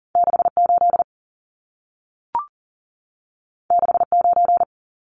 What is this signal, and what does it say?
Morse code, heard as a keyed tone, about 40 wpm, 700 Hz. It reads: 68 69
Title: Numbers - 2-Digit 40wpm